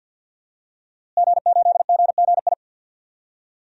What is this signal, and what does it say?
Morse code, heard as a keyed tone, about 47 wpm, 700 Hz. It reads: G8GGI
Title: DXCC-100 Call Signs - 5 Characters - Repeated 2x - 50wpm